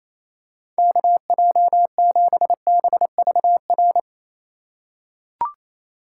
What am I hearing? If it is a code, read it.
Morse code, heard as a keyed tone, about 28 wpm, 700 Hz. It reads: KJ7BVR
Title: USA Call Signs - 6 Characters - 2x Character Spacing - 28wpm